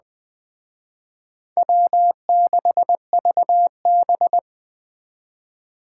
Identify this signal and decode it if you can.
Morse code, heard as a keyed tone, about 20 wpm, 700 Hz. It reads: W6VB